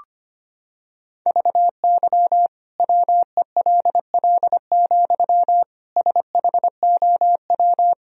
Morse code, heard as a keyed tone, about 25 wpm, 700 Hz. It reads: VY WELL, SHOW
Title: Sentences from Top 200 Words and Abbreviations 25wpm